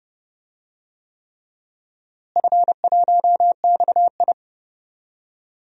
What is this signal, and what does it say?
Morse code, heard as a keyed tone, about 30 wpm, 700 Hz. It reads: F1XI
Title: DXCC-100 Call Signs - 4 Characters - Repeated 3x - 30wpm